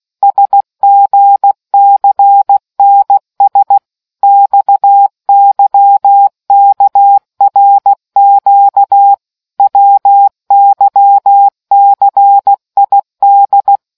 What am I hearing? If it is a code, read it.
Morse code, heard as a keyed tone, about 16 wpm, 800 Hz. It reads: SGCNS XYKRQ WYCID